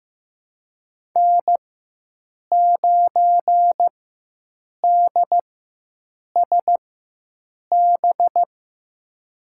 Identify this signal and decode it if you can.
Morse code, heard as a keyed tone, about 15 wpm, spaced as for 7 wpm, 700 Hz. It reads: N9DSB
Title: USA Call Signs - 4x Character Spacing - 15wpm